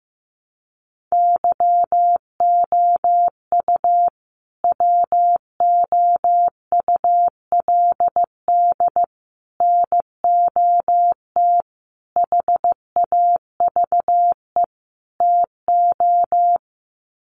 Morse code, heard as a keyed tone, about 15 wpm, 700 Hz. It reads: YOU WOULD NOT HAVE TO
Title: Sets of 5 Words Encore 15wpm